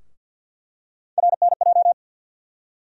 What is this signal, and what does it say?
Morse code, heard as a keyed tone, about 52 wpm, 700 Hz. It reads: RNJ
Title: USA Call Sign Postfixes - 3 Letters 50wpm